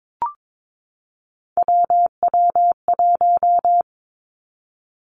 Morse code, heard as a keyed tone, about 22 wpm, 700 Hz. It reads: WW1
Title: Three Challenging Letter-Number Combinations 22wpm